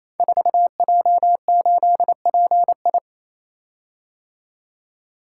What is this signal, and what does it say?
Morse code, heard as a keyed tone, about 29 wpm, 700 Hz. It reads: 4J8PI
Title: Code Groups - 5 Letter:Numbers - 28wpm